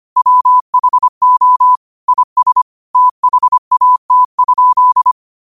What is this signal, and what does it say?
Morse code, heard as a keyed tone, about 25 wpm, 1000 Hz. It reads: WHO IS THAT?
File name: qotdfiles / 25wpm